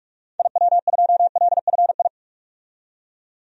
Morse code, heard as a keyed tone, about 42 wpm, 700 Hz. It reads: IW2LFI
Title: DXCC-100 Call Signs - 6 Characters - 45wpm